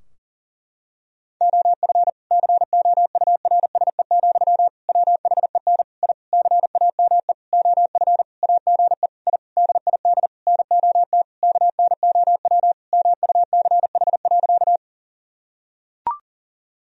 Morse code, heard as a keyed tone, about 37 wpm, 700 Hz. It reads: OF COURSE, WHEN I CAME OF AGE I DID NOT KNOW MUCH.
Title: Sentences from Top 1000 Words - Review Part 3 - 40wpm